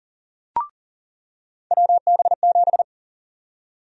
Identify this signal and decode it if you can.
Morse code, heard as a keyed tone, about 38 wpm, 700 Hz. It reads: WB7
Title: USA Call Sign Prefixes - 40wpm